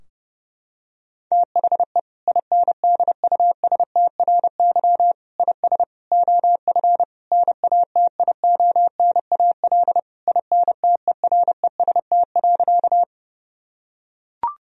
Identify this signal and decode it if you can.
Morse code, heard as a keyed tone, about 30 wpm, 700 Hz. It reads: THE INDUSTRY IS OF NATIONAL INTEREST.
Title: Sentences from Top 1000 Words - Review Part 4 - 30wpm